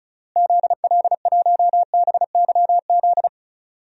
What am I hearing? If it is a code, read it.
Morse code, heard as a keyed tone, about 36 wpm, 700 Hz. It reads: ZL1BYZ
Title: DXCC-100 Call Signs - 6 Characters - Repeated 2x - 35wpm